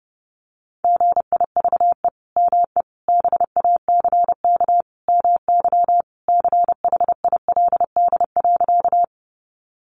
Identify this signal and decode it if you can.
Morse code, heard as a keyed tone, about 30 wpm, 700 Hz. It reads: GIVE ME BACK MY CHILD.
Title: Sentences from Top 800 Words 30wpm